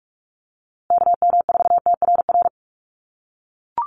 Morse code, heard as a keyed tone, about 44 wpm, 700 Hz. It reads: KM4TRR
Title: USA Call Signs Repeated 2x 45wpm